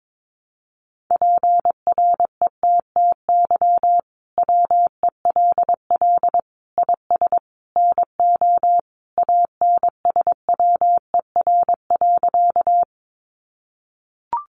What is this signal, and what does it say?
Morse code, heard as a keyed tone, about 22 wpm, 700 Hz. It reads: PRETTY WELL IS NO ANSWER.
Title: Sentences from Top 1000 Words 22wpm